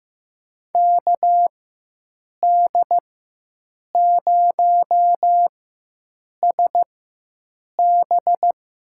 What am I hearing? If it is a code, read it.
Morse code, heard as a keyed tone, about 15 wpm, spaced as for 7 wpm, 700 Hz. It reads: KD0SB
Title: USA Call Signs - 4x Character Spacing - 15wpm